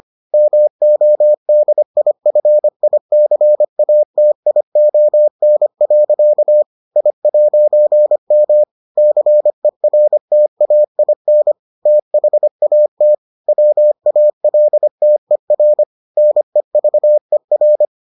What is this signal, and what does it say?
Morse code, heard as a keyed tone, about 25 wpm, 600 Hz. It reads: MODIFICATION. I'M CERTAIN THAT WALTER NEVER